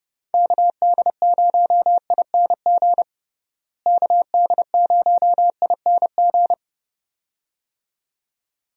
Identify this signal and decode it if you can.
Morse code, heard as a keyed tone, about 30 wpm, 700 Hz. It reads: KD0ING KD0ING
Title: USA Call Signs - 6 Characters - Repeated 2x - 30wpm